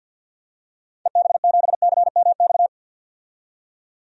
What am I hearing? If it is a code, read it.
Morse code, heard as a keyed tone, about 48 wpm, 700 Hz. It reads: EB7CMX
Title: DXCC-100 Call Signs - 6 Characters - 50wpm